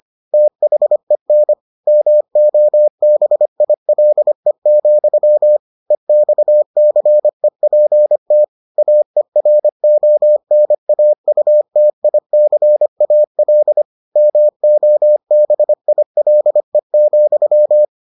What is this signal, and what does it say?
Morse code, heard as a keyed tone, about 25 wpm, 600 Hz. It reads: THEN MOBILE, EXCEPT AERONAUTICAL MOBILE,